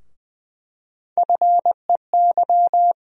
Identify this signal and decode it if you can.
Morse code, heard as a keyed tone, about 20 wpm, 700 Hz. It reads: FEY
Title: USA Call Sign Postfixes - 3 Letters 20wpm